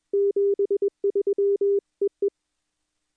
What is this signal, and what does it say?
Morse code, heard as a keyed tone, about 21 wpm, 400 Hz. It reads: 73 EE
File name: contests / 2019-june=vhf-k5tr- -0101 0-2